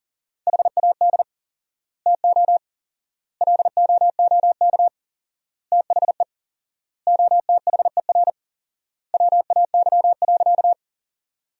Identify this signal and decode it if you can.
Morse code, heard as a keyed tone, about 40 wpm, 700 Hz. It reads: HAD TO LOOK THE OTHER WAY.
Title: 4x Word-Spacing - Sentences from Top 100 Words - 40wpm